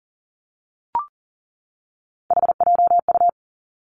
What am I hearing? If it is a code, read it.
Morse code, heard as a keyed tone, about 38 wpm, 700 Hz. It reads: HJU
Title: Three Challenging Letter-Number Combinations 40wpm